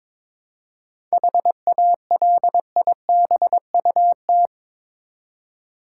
Morse code, heard as a keyed tone, about 22 wpm, 700 Hz. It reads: HALIBUT
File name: Difficult Words with Hints - 22wpm